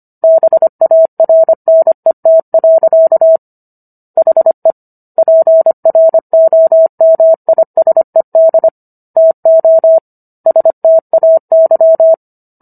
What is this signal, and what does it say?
Morse code, heard as a keyed tone, about 25 wpm, 650 Hz. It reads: BARNET. HE PROMISED TO STAY